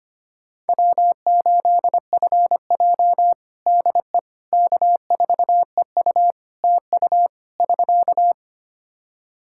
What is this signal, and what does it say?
Morse code, heard as a keyed tone, about 25 wpm, 700 Hz. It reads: W8FJ DE K4EU TU <SK>
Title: Practice QSO - Exchange 4 - Ending - 25wpm